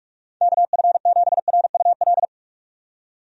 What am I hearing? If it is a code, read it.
Morse code, heard as a keyed tone, about 48 wpm, 700 Hz. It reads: KF7RUL
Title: USA Call Signs - 6 Characters - 45wpm